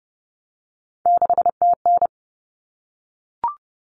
Morse code, heard as a keyed tone, about 30 wpm, 700 Hz. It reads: 6TN
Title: Common Cut Numbers - 3-Digit 30wpm